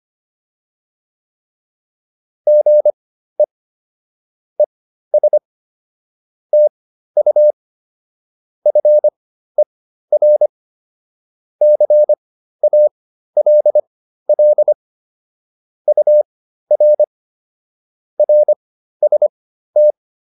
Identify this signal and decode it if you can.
Morse code, heard as a keyed tone, about 25 wpm, spaced as for 13 wpm, 600 Hz. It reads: GE ES TU FER CALL UR RST